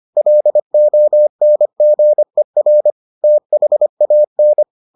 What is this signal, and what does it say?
Morse code, heard as a keyed tone, about 25 wpm, 600 Hz. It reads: LONGER THAN